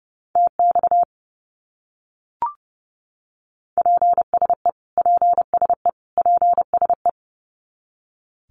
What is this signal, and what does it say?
Morse code, heard as a keyed tone, about 30 wpm, 700 Hz. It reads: TX PSE PSE PSE
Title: Abbreviations Repeated 3x - 30wpm